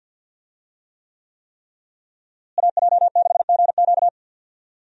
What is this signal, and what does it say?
Morse code, heard as a keyed tone, about 52 wpm, 700 Hz. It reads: AJ6GQ
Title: USA Call Signs - 5 Characters - Repeated 3x - 50wpm